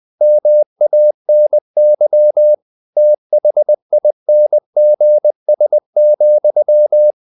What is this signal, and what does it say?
Morse code, heard as a keyed tone, about 20 wpm, 600 Hz. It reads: MANY THINGS,
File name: the old man and the sea part 2 20wpm